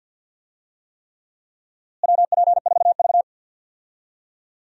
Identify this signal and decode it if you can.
Morse code, heard as a keyed tone, about 52 wpm, 700 Hz. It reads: WP4V
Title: USA Call Signs - 4 Characters - Repeated 3x - 50wpm